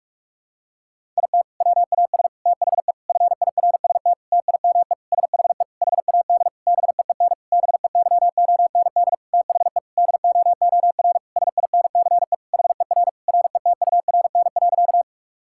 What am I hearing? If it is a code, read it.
Morse code, heard as a keyed tone, about 46 wpm, 700 Hz. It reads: IT WAS THE FIRST TIME SHE HAD BEEN BEYOND THE DOOR SINCE HER RETURN.
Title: Sentences from Top 1000 Words - Review Part 6 - 45wpm